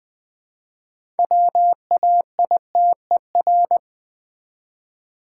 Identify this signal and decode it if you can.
Morse code, heard as a keyed tone, about 20 wpm, 700 Hz. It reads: WAITER
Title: 6 Letter Words - First 3 Letters Given - 20wpm